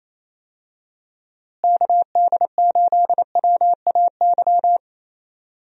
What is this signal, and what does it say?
Morse code, heard as a keyed tone, about 27 wpm, 700 Hz. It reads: KD8WAY
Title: USA Call Signs - 6 Characters - Repeated 2x - 28wpm